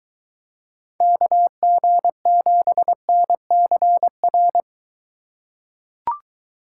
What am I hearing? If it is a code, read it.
Morse code, heard as a keyed tone, about 22 wpm, 700 Hz. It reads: KG7NCR